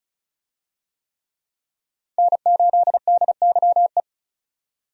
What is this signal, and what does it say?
Morse code, heard as a keyed tone, about 36 wpm, 700 Hz. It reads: N8DYE
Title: USA Call Signs - 5 Characters - 35wpm